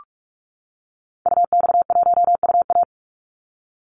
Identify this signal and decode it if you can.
Morse code, heard as a keyed tone, about 44 wpm, 700 Hz. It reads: UX1UA
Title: Call Signs - 45wpm